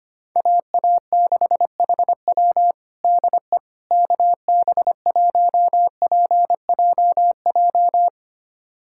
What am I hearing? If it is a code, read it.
Morse code, heard as a keyed tone, about 25 wpm, 700 Hz. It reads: AA6HW DE KB1PJJ
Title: USA Callsign DE Callsign 25wpm